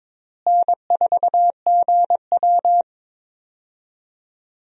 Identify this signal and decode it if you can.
Morse code, heard as a keyed tone, about 22 wpm, 700 Hz. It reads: N4GW
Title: USA Call Signs - 4 Characters - Repeated 2x - 22wpm